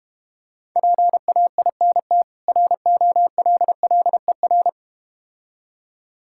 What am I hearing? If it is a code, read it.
Morse code, heard as a keyed tone, about 32 wpm, 700 Hz. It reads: PAINT ROLLER